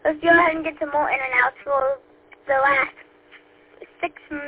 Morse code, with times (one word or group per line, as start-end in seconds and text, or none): none